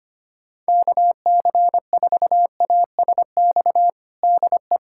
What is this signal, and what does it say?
Morse code, heard as a keyed tone, about 25 wpm, 700 Hz. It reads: KC4ASX DE
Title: USA Callsign DE Callsign 25wpm